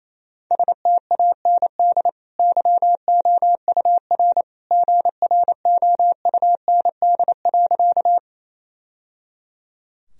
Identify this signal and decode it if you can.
Morse code, heard as a keyed tone, about 28 wpm, 690 Hz. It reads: STAND YOUR GROUND.
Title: Sentences from Top 1000 Words - Review Part 2 - 28wpm